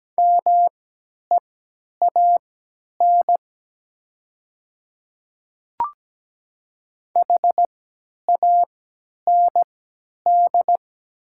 Morse code, heard as a keyed tone, about 17 wpm, spaced as for 10 wpm, 700 Hz. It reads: MEAN HAND
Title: 3x Character Spacing - Top 200 Words - 17wpm